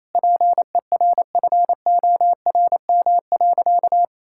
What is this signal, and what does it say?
Morse code, heard as a keyed tone, about 28 wpm, 700 Hz. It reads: PERFORM.
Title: Sentences from Top 1000 Words - Review Part 5 - 28wpm